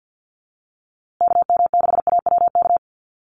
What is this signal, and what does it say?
Morse code, heard as a keyed tone, about 50 wpm, 700 Hz. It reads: KM6AWK
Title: USA Call Signs Repeated 2x 50wpm